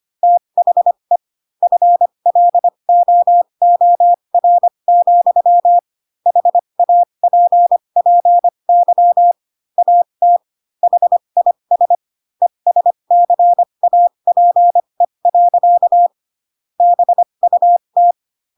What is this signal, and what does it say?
Morse code, heard as a keyed tone, about 25 wpm, spaced as for 22 wpm, 700 Hz. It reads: THE FLOOR, HAPPY AT HIS ESCAPE. BUT
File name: Pinocchio / Pinocchio ch4 18 25 ik1hge